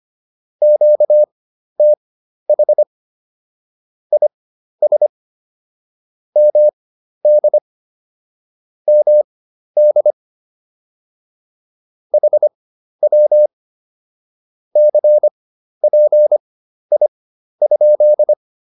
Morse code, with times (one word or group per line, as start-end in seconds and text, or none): 0.6-2.8 QTH
4.1-5.1 IS
6.4-7.6 MD
8.9-10.1 MD
12.1-13.5 HW
14.8-18.3 CPI?